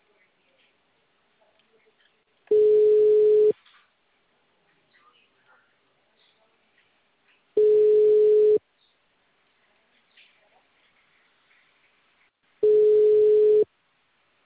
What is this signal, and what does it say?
Ringback tone (Europe, 425 Hz), with speech.
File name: out- -2002- - - .28796